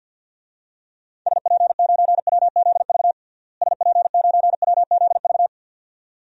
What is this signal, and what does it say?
Morse code, heard as a keyed tone, about 52 wpm, 700 Hz. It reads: SP9WZV SP9WZV
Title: DXCC-100 Call Signs - 6 Characters - Repeated 2x - 50wpm